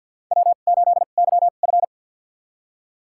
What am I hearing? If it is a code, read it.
Morse code, heard as a keyed tone, about 52 wpm, spaced as for 37 wpm, 690 Hz. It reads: W8YF
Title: USA Call Signs - 2x Character Spacing - 50wpm